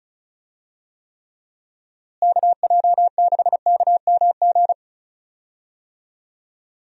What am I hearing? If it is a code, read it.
Morse code, heard as a keyed tone, about 36 wpm, 700 Hz. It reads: KJ6KMG
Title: USA Call Signs - 6 Characters - Repeated 2x - 35wpm